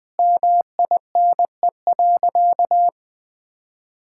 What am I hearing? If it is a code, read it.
Morse code, heard as a keyed tone, about 20 wpm, 700 Hz. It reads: MINE.
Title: Sentences with 3-Word Phrases - 20wpm